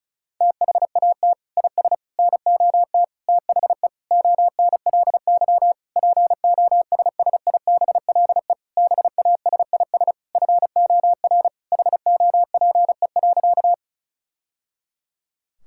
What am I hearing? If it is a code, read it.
Morse code, heard as a keyed tone, about 36 wpm, 700 Hz. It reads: THAT IS NOT THE ONLY POSSIBLE BASIS FOR HOPE.
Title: Sentences from Top 1000 Words - Review Part 5 - 35wpm